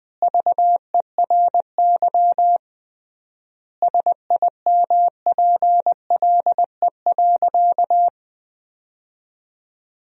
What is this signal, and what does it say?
Morse code, heard as a keyed tone, about 21 wpm, 700 Hz. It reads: VERY SIMPLE.
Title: 3x Word-Spacing - Sentences from Top 500 Words - 20wpm